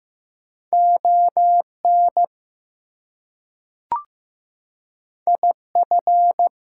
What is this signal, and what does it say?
Morse code, heard as a keyed tone, about 15 wpm, 700 Hz. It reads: ON IF
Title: Two Letter Words - 15wpm